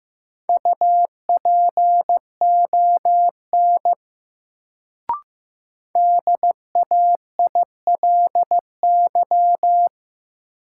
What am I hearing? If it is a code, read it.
Morse code, heard as a keyed tone, about 15 wpm, 700 Hz. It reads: UPON DAILY
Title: Sentences from Top 1100 Words - New Word Drill Alt - 15wpm